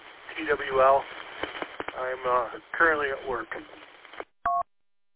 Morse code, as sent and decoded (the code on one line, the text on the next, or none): none